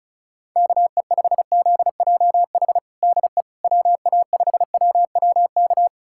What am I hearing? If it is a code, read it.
Morse code, heard as a keyed tone, about 36 wpm, 700 Hz. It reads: KE5ZJH DE WA5WWK
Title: USA Callsign DE Callsign - 6 Characters - 35wpm